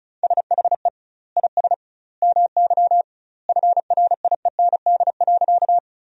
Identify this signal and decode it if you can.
Morse code, heard as a keyed tone, about 34 wpm, 700 Hz. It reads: SHE IS MY FRIEND.
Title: 2x Word-Spacing - Sentences from Top 300 Words - 35wpm